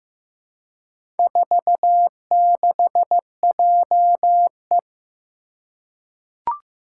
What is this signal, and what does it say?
Morse code, heard as a keyed tone, about 15 wpm, 700 Hz. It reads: 46JE
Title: Four Challenging Letter-Number Combinations 15wpm